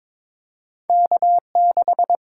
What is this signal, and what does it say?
Morse code, heard as a keyed tone, about 22 wpm, 700 Hz. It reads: K6
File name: practice / 22wpm / Two Letter-Number Combinations - 22wpm